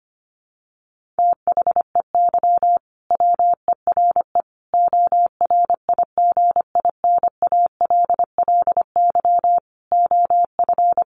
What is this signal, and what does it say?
Morse code, heard as a keyed tone, about 25 wpm, 700 Hz. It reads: THEY WERE ORIGINALLY OF
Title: Sentences from Top 1500 Words 25wpm